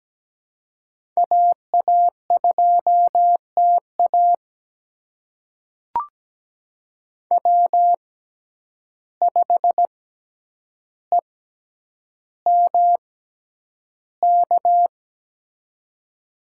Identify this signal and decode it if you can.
Morse code, heard as a keyed tone, about 17 wpm, spaced as for 9 wpm, 700 Hz. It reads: AA2TA W5EMK
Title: USA Call Signs - 5 Characters - 6x Character Spacing - 17wpm